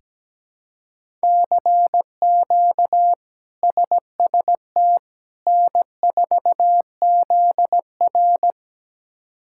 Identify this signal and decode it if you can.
Morse code, heard as a keyed tone, about 17 wpm, 700 Hz. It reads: CQ SST N4ZR
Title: K1USN SST QSO Practice - 17wpm